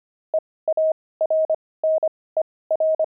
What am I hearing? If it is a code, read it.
Morse code, heard as a keyed tone, about 25 wpm, spaced as for 18 wpm, 630 Hz. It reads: EARNER
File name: pangram-fs-18-cs-25-zeal